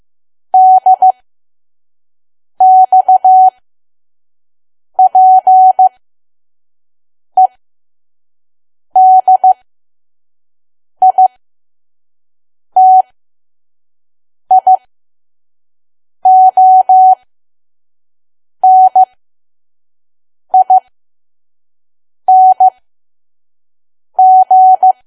Morse code, heard as a keyed tone, about 15 wpm, spaced as for 5 wpm, 750 Hz. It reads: DXPEDITIONING
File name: Archive / 5 WPM files / PM